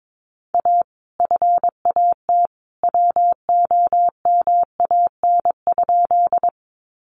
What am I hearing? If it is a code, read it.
Morse code, heard as a keyed tone, about 22 wpm, 700 Hz. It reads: A FAT WOMAN?
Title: Sentences from Top 1100 Words 22wpm